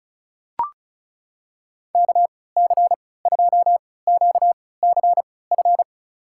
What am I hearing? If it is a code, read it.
Morse code, heard as a keyed tone, about 35 wpm, spaced as for 20 wpm, 700 Hz. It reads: KC2QCF
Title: USA Call Signs - 6 Characters - 3x Character Spacing - 35wpm